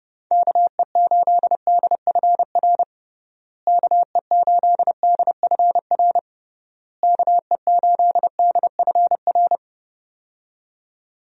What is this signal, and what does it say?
Morse code, heard as a keyed tone, about 30 wpm, 700 Hz. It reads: KE8DFR KE8DFR KE8DFR
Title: USA Call Signs - 6 Characters - Repeated 3x - 30wpm